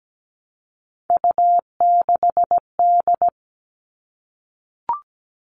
Morse code, heard as a keyed tone, about 17 wpm, 700 Hz. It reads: U6D
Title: All Cut Numbers - 3-Digit 17wpm